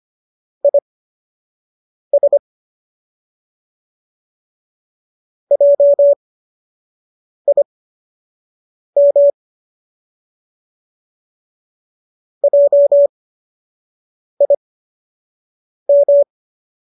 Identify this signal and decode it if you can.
Morse code, heard as a keyed tone, about 25 wpm, spaced as for 6 wpm, 570 Hz. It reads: IS JIM JIM
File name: qso002 06